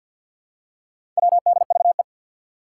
Morse code, heard as a keyed tone, about 51 wpm, 700 Hz. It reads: WDVE
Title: Four Challenging Letter-Number Combinations 50wpm